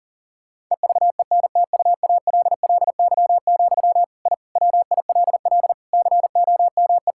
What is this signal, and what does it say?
Morse code, heard as a keyed tone, about 40 wpm, 700 Hz. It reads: EVENTUALLY, I WILL COME
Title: Sentences from Top 1000 Words - Review Part 6 - 40wpm